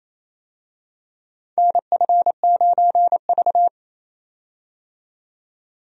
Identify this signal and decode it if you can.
Morse code, heard as a keyed tone, about 28 wpm, 700 Hz. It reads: NF9V
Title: USA Call Signs - 4 Characters - 28wpm